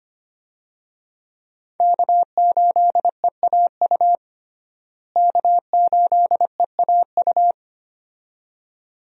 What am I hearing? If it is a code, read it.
Morse code, heard as a keyed tone, about 25 wpm, 700 Hz. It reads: K8EAU K8EAU
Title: USA Call Signs - 5 Characters - Repeated 2x - 25wpm